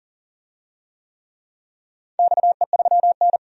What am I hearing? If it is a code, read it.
Morse code, heard as a keyed tone, about 39 wpm, 700 Hz. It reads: XE3N
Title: DXCC-100 Call Signs - 4 Characters - 40wpm